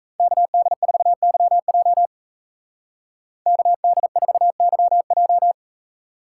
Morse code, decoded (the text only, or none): KD4YJ KD4YJ